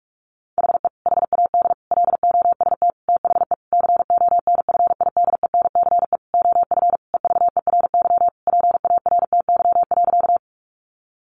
Morse code, heard as a keyed tone, about 44 wpm, 700 Hz. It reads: HE HAD LOST THE CONFIDENCE OF EVERY PARTY.
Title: Sentences from Top 1500 Words 45wpm